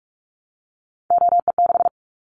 Morse code, heard as a keyed tone, about 46 wpm, 700 Hz. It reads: OE6
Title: Callsign Prefixes for Top 100 DXCC - 45wpm